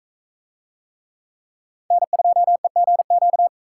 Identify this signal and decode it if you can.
Morse code, heard as a keyed tone, about 40 wpm, 700 Hz. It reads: N2EGQ